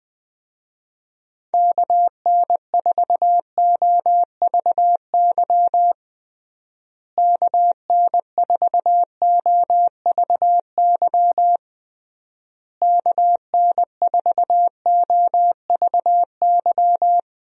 Morse code, heard as a keyed tone, about 21 wpm, 700 Hz. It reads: KN4OVY KN4OVY KN4OVY
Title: USA Call Signs - 6 Characters - Repeated 3x - 20wpm